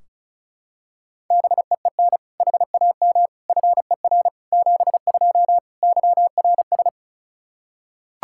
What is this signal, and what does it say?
Morse code, heard as a keyed tone, about 34 wpm, 700 Hz. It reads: BEEN HAM FER 72 YRS
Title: Practice QSO - Exchange 3 - More Info - 35wpm